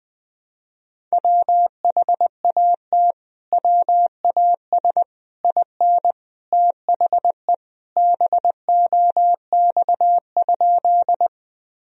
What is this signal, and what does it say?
Morse code, heard as a keyed tone, about 21 wpm, 700 Hz. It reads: WHAT WAS IN THE BOX?
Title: Sentences from Top 1000 Words - Review Part 2 - 20wpm